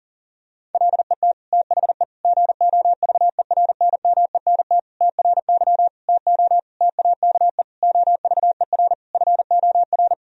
Morse code, heard as a keyed tone, about 40 wpm, 700 Hz. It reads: LET THE GOVERNMENT TRY TO TAKE OVER FOR
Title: Sentences from Top 1000 Words - Review Part 3 - 40wpm